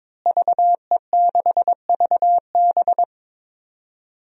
Morse code, heard as a keyed tone, about 22 wpm, 700 Hz. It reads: VE6VB
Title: DXCC-100 Call Signs - 5 Characters - Repeated 3x - 22wpm